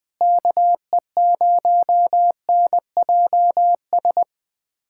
Morse code, heard as a keyed tone, about 20 wpm, 700 Hz. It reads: KE0NJS
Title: USA Callsign DE Callsign 20wpm